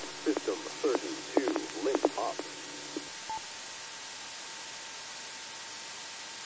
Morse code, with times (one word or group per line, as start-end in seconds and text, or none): none